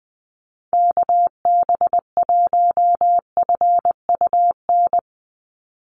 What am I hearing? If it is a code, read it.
Morse code, heard as a keyed tone, about 20 wpm, 700 Hz. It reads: KB1FUN
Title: USA Call Signs 20wpm